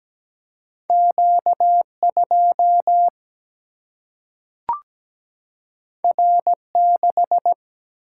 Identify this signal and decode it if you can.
Morse code, heard as a keyed tone, about 17 wpm, 700 Hz. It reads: Q2 R6
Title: Single-Letter Single-Number 17wpm